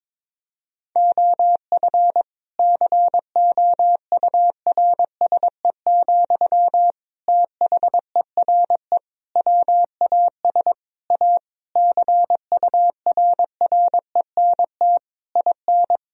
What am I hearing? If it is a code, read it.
Morse code, heard as a keyed tone, about 22 wpm, 700 Hz. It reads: OF COURSE, THERE WAS A CURRENT IN
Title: Sentences from Top 1000 Words - Review Part 4 - 22wpm